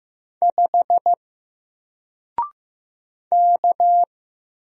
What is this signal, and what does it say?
Morse code, heard as a keyed tone, about 15 wpm, 700 Hz. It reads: 5 K
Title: Single Character (Letter:Number:Punctuation:Prosign) Alt 15wpm